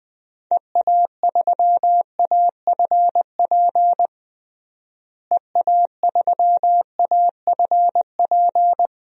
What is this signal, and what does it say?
Morse code, heard as a keyed tone, about 19 wpm, 700 Hz. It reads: EA3AFP EA3AFP
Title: DXCC-100 Call Signs - 6 Characters - Repeated 2x - 20wpm